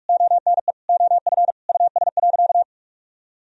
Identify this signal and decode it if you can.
Morse code, heard as a keyed tone, about 46 wpm, 700 Hz. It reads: ONE OF US.
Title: Sentences from Top 1000 Words - Review Part 4 - 45wpm